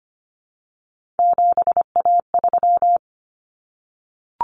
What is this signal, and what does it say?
Morse code, heard as a keyed tone, about 25 wpm, 700 Hz. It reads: 7A3
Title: Common Cut Numbers - 3-Digit 25wpm